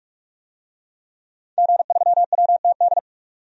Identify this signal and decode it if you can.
Morse code, heard as a keyed tone, about 46 wpm, 700 Hz. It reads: G3WTD
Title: DXCC-100 Call Signs - 5 Characters - 45wpm